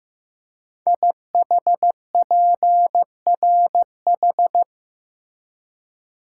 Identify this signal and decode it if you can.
Morse code, heard as a keyed tone, about 15 wpm, 700 Hz. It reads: IHPRH
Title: Code Groups - 5 Letters - 15wpm